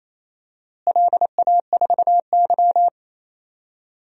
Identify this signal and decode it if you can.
Morse code, heard as a keyed tone, about 27 wpm, 700 Hz. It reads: LA4Y
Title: DXCC-100 Call Signs - 4 Characters - Repeated 3x - 28wpm